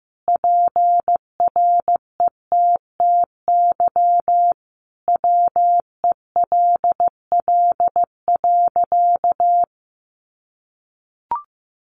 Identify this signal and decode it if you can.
Morse code, heard as a keyed tone, about 15 wpm, 700 Hz. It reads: PRETTY WELL.
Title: Sentences from Top 1000 Words 15wpm